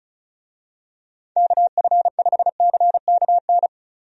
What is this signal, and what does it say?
Morse code, heard as a keyed tone, about 36 wpm, 700 Hz. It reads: KF5CKN
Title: USA Call Signs - 6 Characters - Repeated 2x - 35wpm